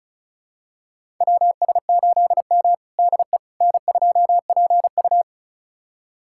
Morse code, heard as a keyed tone, about 36 wpm, 700 Hz. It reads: WS8M DE N2PU
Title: USA Callsign DE Callsign - 4 Characters - 35wpm